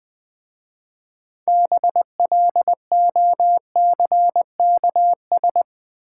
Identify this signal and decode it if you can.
Morse code, heard as a keyed tone, about 20 wpm, 690 Hz. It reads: BLOCKS
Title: Easy Words with Hints - 20wpm